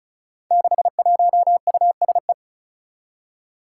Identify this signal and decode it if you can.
Morse code, heard as a keyed tone, about 36 wpm, 700 Hz. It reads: 61USE
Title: Five Challenging Letter-Number Combinations 35wpm